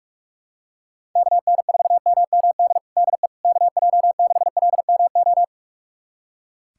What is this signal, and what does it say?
Morse code, heard as a keyed tone, about 46 wpm, 700 Hz. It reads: KN4MMD DE KJ6LMO
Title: USA Callsign DE Callsign - 6 Characters - 45wpm